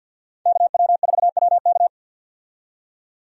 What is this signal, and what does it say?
Morse code, heard as a keyed tone, about 46 wpm, 700 Hz. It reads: KW4WK
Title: USA Call Signs - 5 Characters - 50wpm